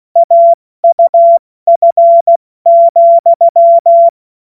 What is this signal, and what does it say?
Morse code, heard as a keyed tone, about 16 wpm, 680 Hz. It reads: AUF,
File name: Morsekurs / Witze / Witz 13 Mai 2025 70 F80